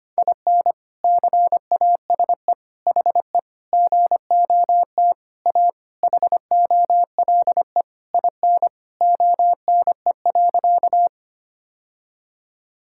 Morse code, heard as a keyed tone, about 25 wpm, 700 Hz. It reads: IN CASE HE GOT A HOLE IN ONE.
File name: practice / 25wpm / All-Age Appropriate Jokes - 25wpm